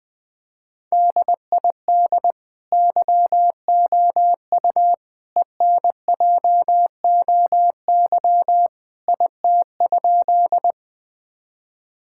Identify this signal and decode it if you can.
Morse code, heard as a keyed tone, about 20 wpm, 700 Hz. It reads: DID YOU ENJOY IT?
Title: Sentences from Top 1000 Words - Review Part 5 - 20wpm